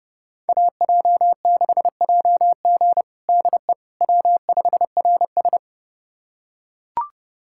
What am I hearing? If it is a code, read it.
Morse code, heard as a keyed tone, about 32 wpm, 700 Hz. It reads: AJ6JG DE W5RS
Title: USA Callsign DE Callsign 30wpm